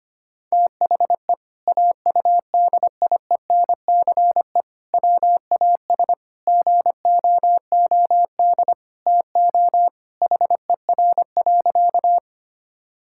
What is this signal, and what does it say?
Morse code, heard as a keyed tone, about 25 wpm, 700 Hz. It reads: THE AUDIENCE WAS GOOD TO HER.
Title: Sentences from Top 1000 Words - Review Part 5 - 25wpm